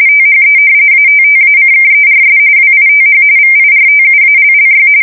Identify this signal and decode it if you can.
RTTY, about 45.45 baud. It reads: THE ATTACKERS WITH VIRTUALLY
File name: fsk170 / battleshipnj